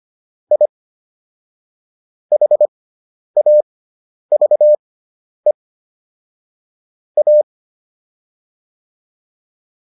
Morse code, heard as a keyed tone, about 25 wpm, spaced as for 10 wpm, 600 Hz. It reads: I HAVE A